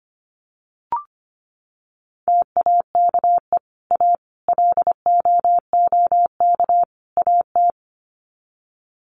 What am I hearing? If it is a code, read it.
Morse code, heard as a keyed tone, about 25 wpm, 700 Hz. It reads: TAKE A LOOK AT
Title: Sets of 4 Words 25wpm